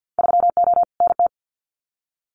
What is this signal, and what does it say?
Morse code, heard as a keyed tone, about 50 wpm, 700 Hz. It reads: 3O NT
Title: Winter Field Day Exchanges Encore 50wpm